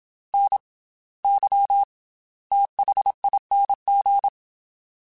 Morse code, heard as a keyed tone, about 27 wpm, 800 Hz. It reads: N Y THING
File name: Special INGs / Special INGs-mp3s / Slow ing @25